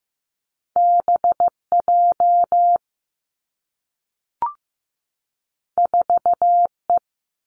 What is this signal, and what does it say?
Morse code, heard as a keyed tone, about 15 wpm, 700 Hz. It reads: B J 4 E
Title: Two Challenging Letter-Number Combinations 15wpm